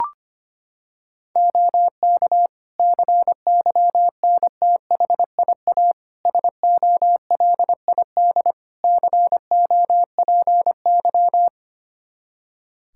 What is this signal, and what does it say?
Morse code, heard as a keyed tone, about 25 wpm, 700 Hz. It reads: OK CYNTHIA SOLID COPY
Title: K1USN SST QSO Practice 25wpm